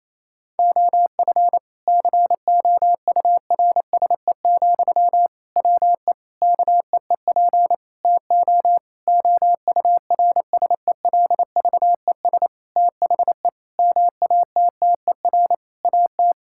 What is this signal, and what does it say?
Morse code, heard as a keyed tone, about 28 wpm, 700 Hz. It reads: OF COURSE, WE KEEP TO OURSELVES THE MATTER AT
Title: Sentences from Top 1600 Words 28wpm